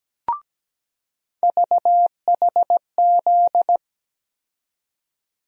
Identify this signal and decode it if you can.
Morse code, heard as a keyed tone, about 17 wpm, 700 Hz. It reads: VHZ
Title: USA Call Sign Postfixes - 3 Letters 17wpm